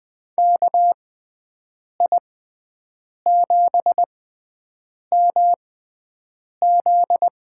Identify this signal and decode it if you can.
Morse code, heard as a keyed tone, about 20 wpm, spaced as for 7 wpm, 700 Hz. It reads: KI7MZ
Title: USA Call Signs - 6 Characters - 6x Character Spacing - 20wpm